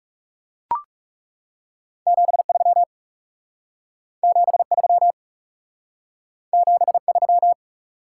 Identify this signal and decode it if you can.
Morse code, heard as a keyed tone, about 37 wpm, 700 Hz. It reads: 73 73 73
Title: Speed-Racing Top 180 QSO Elements 30wpm